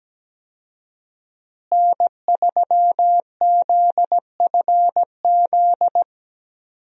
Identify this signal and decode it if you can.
Morse code, heard as a keyed tone, about 17 wpm, 700 Hz. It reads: N3ZFZ
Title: USA Call Signs - 5 Characters - Repeated 2x - 17wpm